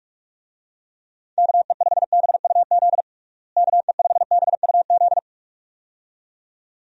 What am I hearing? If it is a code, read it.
Morse code, heard as a keyed tone, about 42 wpm, 700 Hz. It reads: KE5BUZ KE5BUZ
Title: USA Call Signs - 6 Characters - Repeated 2x - 45wpm